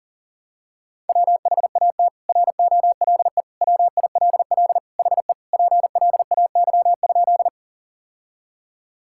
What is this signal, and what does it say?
Morse code, heard as a keyed tone, about 37 wpm, 700 Hz. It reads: WHAT ROLE WILL HE PLAY?
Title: Sentences from Top 1000 Words - Review Part 6 - 40wpm